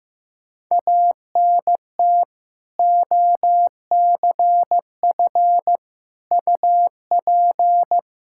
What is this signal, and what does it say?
Morse code, heard as a keyed tone, about 15 wpm, 700 Hz. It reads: ANT OCF UP
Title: Practice QSO - Exchange 2 - Info - 15wpm